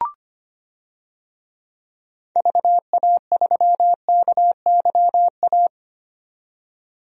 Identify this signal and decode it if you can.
Morse code, heard as a keyed tone, about 25 wpm, 700 Hz. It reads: VA3KYA
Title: DXCC-100 Call Signs - 6 Characters - 25wpm